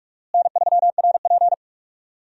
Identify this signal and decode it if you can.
Morse code, heard as a keyed tone, about 46 wpm, 700 Hz. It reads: N3RP
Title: USA Call Signs - 4 Characters - Repeated 2x - 45wpm